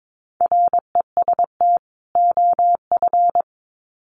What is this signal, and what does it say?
Morse code, heard as a keyed tone, about 22 wpm, 700 Hz. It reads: REST OF
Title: Sets of 2 Words 22wpm